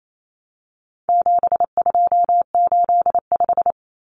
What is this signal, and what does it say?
Morse code, heard as a keyed tone, about 28 wpm, 700 Hz. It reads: 7285
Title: Common Cut Numbers - 4-Digit 28wpm